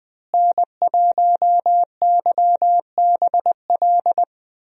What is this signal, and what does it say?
Morse code, heard as a keyed tone, about 20 wpm, 700 Hz. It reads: N1YBL
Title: USA Call Signs - 5 Characters - Repeated 2x - 20wpm